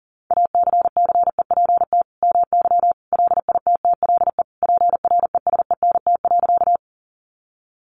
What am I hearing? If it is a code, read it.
Morse code, heard as a keyed tone, about 40 wpm, 700 Hz. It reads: ACCEPT MY LITTLE PRESENT.
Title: Sentences from Top 1300 Words 40wpm